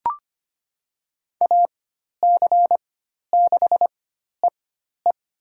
Morse code, heard as a keyed tone, about 25 wpm, spaced as for 12 wpm, 700 Hz. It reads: AC6EE
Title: USA Call Signs - 5 Characters - 4x Character Spacing - 25wpm